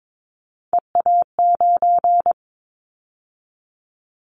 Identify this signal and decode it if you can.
Morse code, heard as a keyed tone, about 22 wpm, 700 Hz. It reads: EA9
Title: Callsign Prefixes for Top 100 DXCC - 22wpm